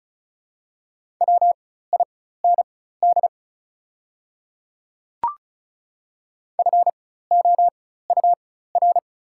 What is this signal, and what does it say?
Morse code, heard as a keyed tone, about 36 wpm, spaced as for 16 wpm, 690 Hz. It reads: WIND FOUR
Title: 4x Character Spacing - Top 500 Words - 35wpm